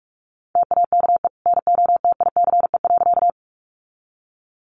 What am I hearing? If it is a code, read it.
Morse code, heard as a keyed tone, about 46 wpm, 700 Hz. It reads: TAKE NOTICE.
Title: Sentences from Top 400 Words 45wpm